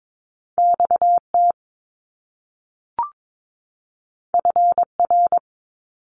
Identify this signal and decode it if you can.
Morse code, heard as a keyed tone, about 22 wpm, 700 Hz. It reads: XT FR
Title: Top Two Letter Combinations in Words 22wpm